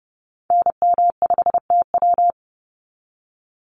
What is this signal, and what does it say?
Morse code, heard as a keyed tone, about 30 wpm, 700 Hz. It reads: NM5TW
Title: NA SOTA Activator Callsigns 30wpm